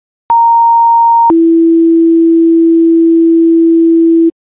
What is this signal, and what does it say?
Two-tone page, heard as a one-second tone, then a three-second tone, about 930 then 330 Hz. It is heard from 0.3 to 4.3 s.